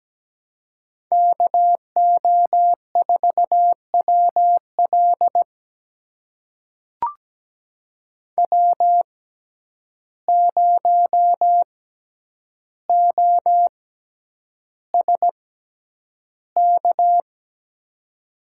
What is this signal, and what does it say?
Morse code, heard as a keyed tone, about 17 wpm, spaced as for 9 wpm, 700 Hz. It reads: KO4WL W0OSK
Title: USA Call Signs - 5 Characters - 6x Character Spacing - 17wpm